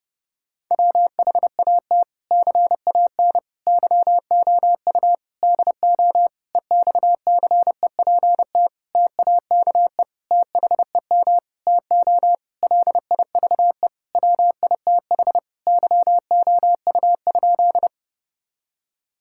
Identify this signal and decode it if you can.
Morse code, heard as a keyed tone, about 30 wpm, 700 Hz. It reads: WHAT CAN YOU DO EXCEPT TAKE THEM TO LIVE WITH YOU?
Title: Sentences from Top 1600 Words 30wpm